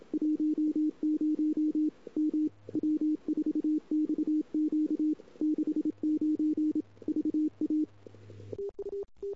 Morse code, heard as a keyed tone, about 31 wpm, 310 Hz. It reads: 10 M W4XQ 69 VA
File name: CW / 2008-ss-cw-k5tr-op-n5rz -21:00